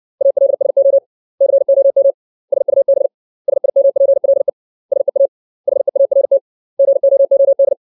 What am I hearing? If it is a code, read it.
Morse code, heard as a keyed tone, about 60 wpm, 550 Hz. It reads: ALSO COM SUB SIMPLE SEA HEART GOOD
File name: 632 words / 632 Words 60WPM